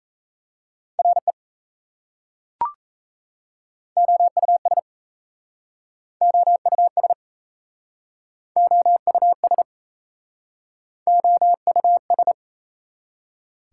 Morse code, heard as a keyed tone, about 38 wpm, 700 Hz. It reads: RE OUS OUS OUS OUS